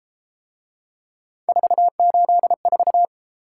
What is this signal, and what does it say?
Morse code, heard as a keyed tone, about 31 wpm, 700 Hz. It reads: 484